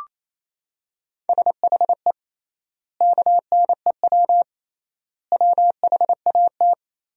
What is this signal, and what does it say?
Morse code, heard as a keyed tone, about 28 wpm, 700 Hz. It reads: SHE KNEW WHAT
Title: 3x Word-Spacing - Sentences from Top 400 Words - 28wpm